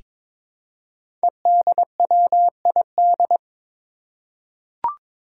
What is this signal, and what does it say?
Morse code, heard as a keyed tone, about 21 wpm, 700 Hz. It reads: EDWID
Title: Five Challenging Letter-Number Combinations 22wpm